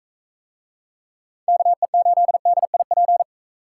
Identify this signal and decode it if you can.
Morse code, heard as a keyed tone, about 40 wpm, 700 Hz. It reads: KE8DIP